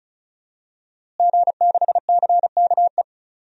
Morse code, heard as a keyed tone, about 35 wpm, 700 Hz. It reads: G6CKE
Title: DXCC-100 Call Signs - 5 Characters - Repeated 2x - 35wpm